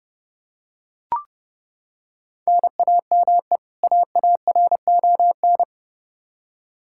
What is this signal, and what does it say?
Morse code, heard as a keyed tone, about 31 wpm, 700 Hz. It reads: NAME AARON
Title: Practice QSO - Exchange 1 - Beginning - 30wpm